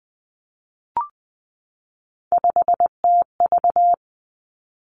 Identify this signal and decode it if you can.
Morse code, heard as a keyed tone, about 19 wpm, 700 Hz. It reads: 5TV
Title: Three Challenging Letter-Number Combinations 20wpm